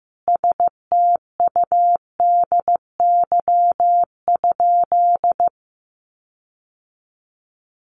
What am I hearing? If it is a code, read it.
Morse code, heard as a keyed tone, about 15 wpm, 700 Hz. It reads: STUDY?
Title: Sentences from Top 1400 Words 15wpm